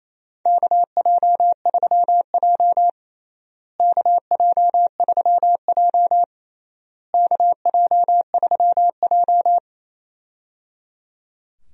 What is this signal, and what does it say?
Morse code, heard as a keyed tone, about 27 wpm, 700 Hz. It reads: KJ3J KJ3J KJ3J
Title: USA Call Signs - 4 Characters - Repeated 3x - 28wpm